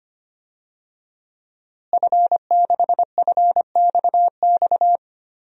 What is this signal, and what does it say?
Morse code, heard as a keyed tone, about 25 wpm, 700 Hz. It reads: F6FXX
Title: DXCC-100 Call Signs - 5 Characters - Repeated 3x - 25wpm